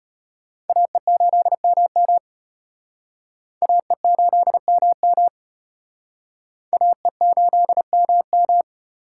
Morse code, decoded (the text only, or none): AE8MM AE8MM AE8MM